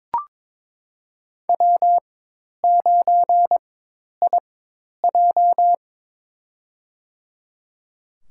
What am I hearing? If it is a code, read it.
Morse code, heard as a keyed tone, about 22 wpm, spaced as for 10 wpm, 700 Hz. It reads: W9IJ
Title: USA Call Signs - 4 Characters - 4x Character Spacing - 22wpm